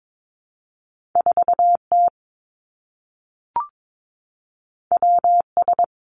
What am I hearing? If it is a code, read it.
Morse code, heard as a keyed tone, about 22 wpm, 700 Hz. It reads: 4 T W S
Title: Two Challenging Letter-Number Combinations 22wpm